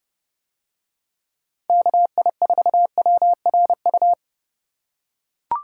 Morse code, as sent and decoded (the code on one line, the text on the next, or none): -.- .. ....- .-- .-. ..-
KI4WRU